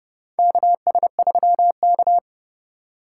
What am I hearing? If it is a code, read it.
Morse code, heard as a keyed tone, about 30 wpm, 700 Hz. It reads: KS3K
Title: USA Call Signs - 2x Character Spacing - 30wpm